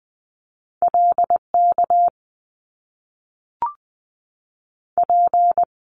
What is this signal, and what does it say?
Morse code, heard as a keyed tone, about 20 wpm, 700 Hz. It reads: LK P
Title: Top Two Letter Combinations in Words 20wpm